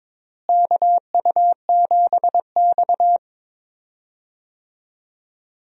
Morse code, heard as a keyed tone, about 22 wpm, 700 Hz. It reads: KU7X
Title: USA Call Signs - 4 Characters - Repeated 2x - 22wpm